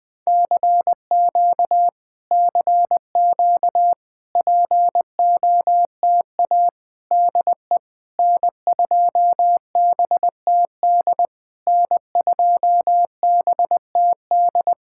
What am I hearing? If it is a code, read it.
Morse code, heard as a keyed tone, about 20 wpm, 690 Hz. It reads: CQ CQ POTA DE N2BTD N2BTD
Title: POTA QSO Practice - 20wpm